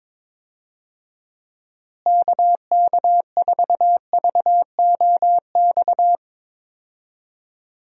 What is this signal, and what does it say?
Morse code, heard as a keyed tone, about 22 wpm, 700 Hz. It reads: KK4VOX
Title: USA Call Signs - 6 Characters - Repeated 2x - 22wpm